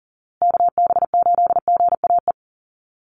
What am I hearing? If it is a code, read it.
Morse code, heard as a keyed tone, about 40 wpm, 700 Hz. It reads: KB8GAE
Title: Call Signs Repeated 2x - 40wpm